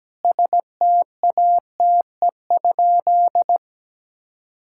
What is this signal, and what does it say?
Morse code, heard as a keyed tone, about 17 wpm, 700 Hz. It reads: STATE?
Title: Sentences from Top 1000 Words - Review Part 5 - 17wpm